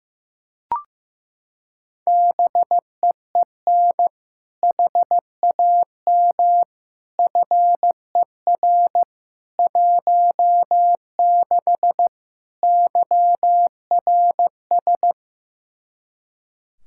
Morse code, heard as a keyed tone, about 15 wpm, 700 Hz. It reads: BEEN HAM FER 16 YRS
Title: K1USN SST QSO Practice 15wpm